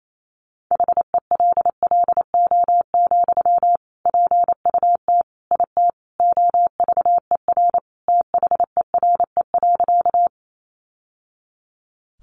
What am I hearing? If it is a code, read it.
Morse code, heard as a keyed tone, about 28 wpm, 700 Hz. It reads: HELLO, PUT IT OVER THERE.
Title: Sentences from Top 1100 Words 28wpm